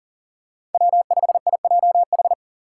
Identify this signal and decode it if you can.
Morse code, heard as a keyed tone, about 40 wpm, 690 Hz. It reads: W5IJH
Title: K1USN SST QSO Practice - 40wpm